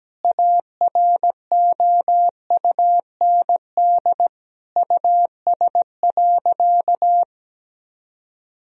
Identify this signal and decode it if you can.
Morse code, heard as a keyed tone, about 17 wpm, 700 Hz. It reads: AROUND US.
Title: Sentences from Top 1000 Words - Review Part 3 - 17wpm